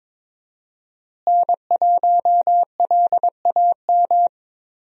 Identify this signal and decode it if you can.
Morse code, heard as a keyed tone, about 21 wpm, 700 Hz. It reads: N1LAM
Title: USA Call Signs - 5 Characters - Repeated 3x - 22wpm